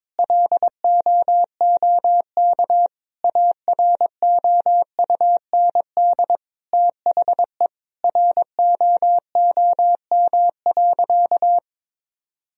Morse code, heard as a keyed tone, about 22 wpm, 700 Hz. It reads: LOOK AROUND THE ROOM.
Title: Sentences from Top 1000 Words - Review Part 3 - 22wpm